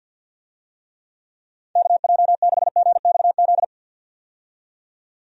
Morse code, heard as a keyed tone, about 52 wpm, 700 Hz. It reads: KJ6GXZ
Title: USA Call Signs - 6 Characters - 50wpm